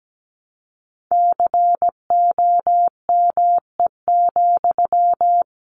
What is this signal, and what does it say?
Morse code, heard as a keyed tone, about 17 wpm, 700 Hz. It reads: COME,
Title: Sentences from Top 700 Words 17wpm